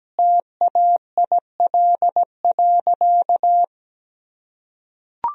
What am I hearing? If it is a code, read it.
Morse code, heard as a keyed tone, about 17 wpm, 700 Hz. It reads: TAIL.
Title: Sentences from Top 1000 Words - Review Part 3 - 17wpm